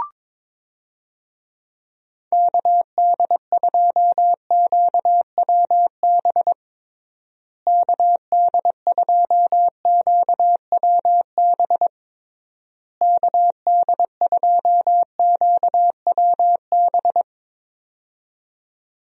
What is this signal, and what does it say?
Morse code, heard as a keyed tone, about 22 wpm, 700 Hz. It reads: KD2QWB KD2QWB KD2QWB
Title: USA Call Signs - 6 Characters - Repeated 3x - 22wpm